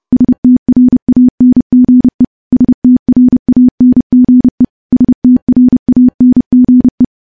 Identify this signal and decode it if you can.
Morse code, heard as a keyed tone, about 30 wpm, 250 Hz. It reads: STRANGE STRANGE STRANGE
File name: morse / morse-training-30WPM